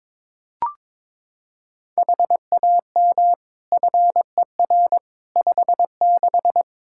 Morse code, heard as a keyed tone, about 22 wpm, 700 Hz. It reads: HAM FER 56
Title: Practice QSO - Exchange 3 - More Info - 22wpm